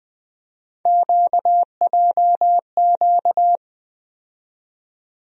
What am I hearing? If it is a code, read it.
Morse code, heard as a keyed tone, about 20 wpm, 700 Hz. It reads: QJQ
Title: USA Call Sign Postfixes - 3 Letters 20wpm